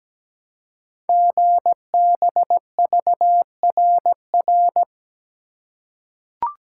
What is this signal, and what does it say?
Morse code, heard as a keyed tone, about 17 wpm, 700 Hz. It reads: GBVRR
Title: Code Groups - 5 Letters - 17wpm